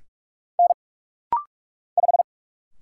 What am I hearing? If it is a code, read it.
Morse code, heard as a keyed tone, about 45 wpm, 700 Hz. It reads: N 5
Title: Single Letter-Number Alt - Rapid-Fire 45wpm